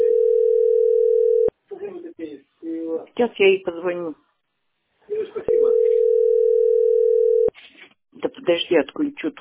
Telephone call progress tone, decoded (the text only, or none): ringback tone